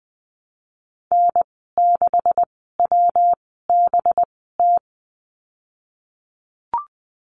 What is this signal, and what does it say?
Morse code, heard as a keyed tone, about 20 wpm, spaced as for 15 wpm, 700 Hz. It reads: N6WBT
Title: 193 Lesson 31g - Letter v - Call Signs 20wpm